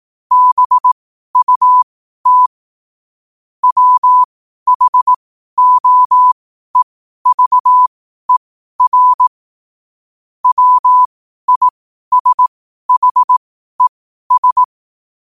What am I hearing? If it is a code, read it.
Morse code, heard as a keyed tone, about 18 wpm, spaced as for 12 wpm, 1000 Hz. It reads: BUT WHOEVER WISHES
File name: qotdfiles / 12wpm